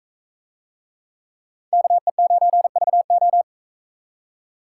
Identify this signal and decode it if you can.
Morse code, heard as a keyed tone, about 40 wpm, 700 Hz. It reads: KE9VO